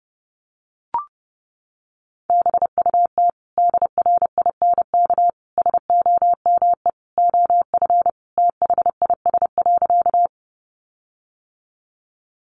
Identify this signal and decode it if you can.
Morse code, heard as a keyed tone, about 30 wpm, 700 Hz. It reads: BUT DRINK SOME OF THIS.
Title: Sentences from Top 1500 Words 30wpm